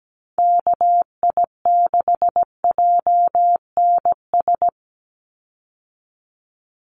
Morse code, heard as a keyed tone, about 17 wpm, 700 Hz. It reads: KI6JNS
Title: USA Call Signs 17wpm